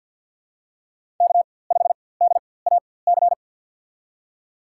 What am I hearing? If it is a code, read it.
Morse code, heard as a keyed tone, about 52 wpm, spaced as for 23 wpm, 700 Hz. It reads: K5DAC
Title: USA Call Signs - 5 Characters - 4x Character Spacing - 50wpm